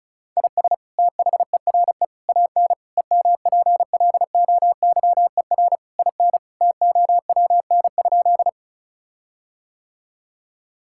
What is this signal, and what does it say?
Morse code, heard as a keyed tone, about 36 wpm, 700 Hz. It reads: IS THERE AN EMPLOYER IN TOWN?
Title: Sentences from Top 1600 Words 35wpm